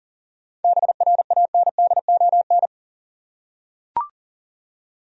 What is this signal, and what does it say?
Morse code, heard as a keyed tone, about 37 wpm, 700 Hz. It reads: BRANDON
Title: USA Common Names - 40wpm